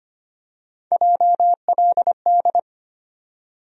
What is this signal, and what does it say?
Morse code, heard as a keyed tone, about 25 wpm, 700 Hz. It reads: JLD
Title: USA Call Sign Postfixes - 3 Letters 25wpm